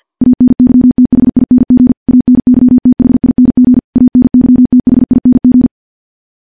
Morse code, heard as a keyed tone, about 50 wpm, 250 Hz. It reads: ANYTHING ANYTHING ANYTHING
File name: morse-training-50WPM